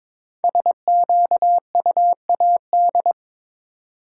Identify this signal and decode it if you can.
Morse code, heard as a keyed tone, about 22 wpm, 690 Hz. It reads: SQUAD
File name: 5 Letter Words - First 3 Letters Given - 22wpm